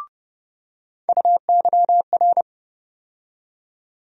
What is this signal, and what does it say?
Morse code, heard as a keyed tone, about 30 wpm, 700 Hz. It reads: UYR
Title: USA Call Sign Postfixes - 3 Letters 30wpm